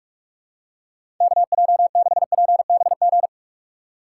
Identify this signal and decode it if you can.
Morse code, heard as a keyed tone, about 46 wpm, 700 Hz. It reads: KJ6PBG
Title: USA Call Signs - 6 Characters - 2x Character Spacing - 45wpm